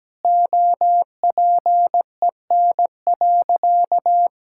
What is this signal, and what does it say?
Morse code, heard as a keyed tone, about 17 wpm, 700 Hz. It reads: OPEN.
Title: Sentences from Top 1600 Words 17wpm